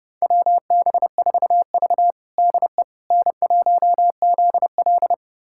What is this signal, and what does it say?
Morse code, heard as a keyed tone, about 30 wpm, 700 Hz. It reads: WB4V DE N1ZL
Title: USA Callsign DE Callsign - 4 Characters - 30wpm